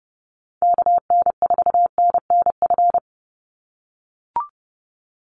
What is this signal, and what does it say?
Morse code, heard as a keyed tone, about 30 wpm, 700 Hz. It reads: KN4NNF
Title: USA Call Signs 30wpm